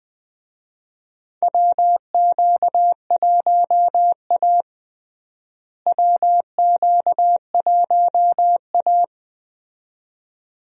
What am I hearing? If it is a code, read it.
Morse code, heard as a keyed tone, about 20 wpm, 700 Hz. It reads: WQ1A WQ1A
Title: USA Call Signs - 4 Characters - Repeated 2x - 20wpm